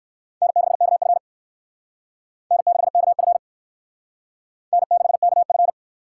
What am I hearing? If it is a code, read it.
Morse code, heard as a keyed tone, about 58 wpm, 700 Hz. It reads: N6KF N6KF N6KF